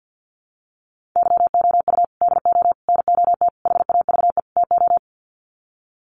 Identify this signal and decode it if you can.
Morse code, heard as a keyed tone, about 50 wpm, 700 Hz. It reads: YOU DO NOT HAVE TO
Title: Sets of 5 Words 50wpm